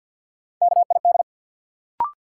Morse code, heard as a keyed tone, about 48 wpm, 700 Hz. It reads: KID
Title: Sets of 5 Words - New Word Drill - 50wpm